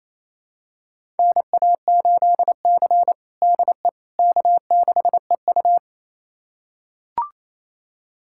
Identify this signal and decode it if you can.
Morse code, heard as a keyed tone, about 28 wpm, 700 Hz. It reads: NA8C DE K6EU
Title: USA Callsign DE Callsign - 4 Characters - 28wpm